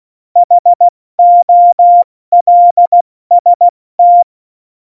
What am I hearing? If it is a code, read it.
Morse code, heard as a keyed tone, about 16 wpm, 700 Hz. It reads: HOLST
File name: Morsekurs / Witze / Witz 18. September 2018, 70 F80